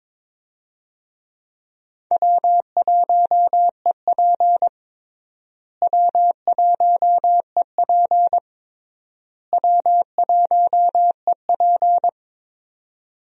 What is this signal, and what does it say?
Morse code, heard as a keyed tone, about 22 wpm, 700 Hz. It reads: W1EP W1EP W1EP
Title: USA Call Signs - 4 Characters - Repeated 3x - 22wpm